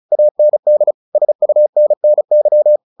Morse code, heard as a keyed tone, about 34 wpm, 600 Hz. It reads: AND SUNNY
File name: the old man and the sea / the old man and the sea 1 35wpm